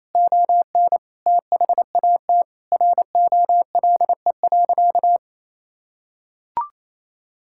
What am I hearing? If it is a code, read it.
Morse code, heard as a keyed tone, about 28 wpm, 700 Hz. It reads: ON THAT ROLE.
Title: Sentences from Top 1000 Words - Review Part 6 - 28wpm